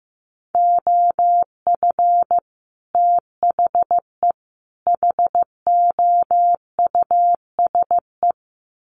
Morse code, heard as a keyed tone, about 15 wpm, 700 Hz. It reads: OF THE HOUSE
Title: Sets of 3 Words 15wpm